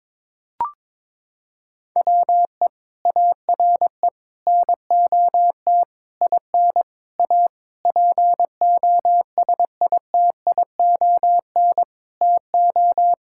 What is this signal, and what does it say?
Morse code, heard as a keyed tone, about 22 wpm, 700 Hz. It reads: WE ARE NOT IN A POSITION TO
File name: Sentences with 7-Word Phrases - 22wpm